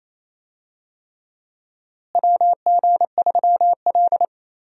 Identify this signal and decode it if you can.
Morse code, heard as a keyed tone, about 28 wpm, 700 Hz. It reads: WG3L
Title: USA Call Signs - 4 Characters - Repeated 3x - 28wpm